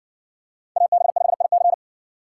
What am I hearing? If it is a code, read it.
Morse code, heard as a keyed tone, about 56 wpm, 700 Hz. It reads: AB5IC